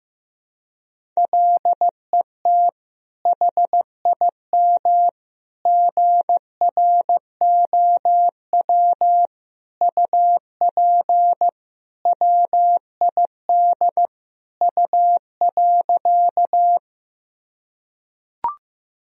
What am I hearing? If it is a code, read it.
Morse code, heard as a keyed tone, about 15 wpm, 700 Hz. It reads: LET HIM GROW UP WID U.
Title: Sentences from Top 300 Words and Abbreviations 15wpm